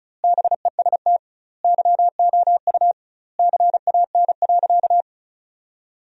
Morse code, heard as a keyed tone, about 36 wpm, 700 Hz. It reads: BEST YOU CAN.
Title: 2x Word-Spacing - Sentences from Top 500 Words - 35wpm